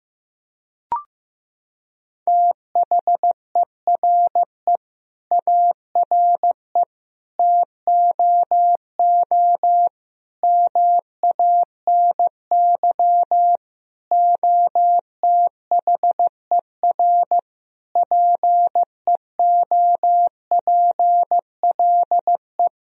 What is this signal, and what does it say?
Morse code, heard as a keyed tone, about 15 wpm, 700 Hz. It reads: THERE ARE TOO MANY OTHER PEOPLE
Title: Sentences from Top 1000 Words - Review Part 3 - 15wpm